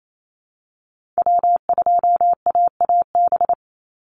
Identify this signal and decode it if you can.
Morse code, heard as a keyed tone, about 28 wpm, 700 Hz. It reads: W2AAB
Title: Call Signs Repeated 2x - 28wpm